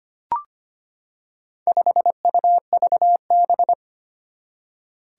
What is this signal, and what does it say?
Morse code, heard as a keyed tone, about 25 wpm, 700 Hz. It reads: 5UVB
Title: Four Challenging Letter-Number Combinations 25wpm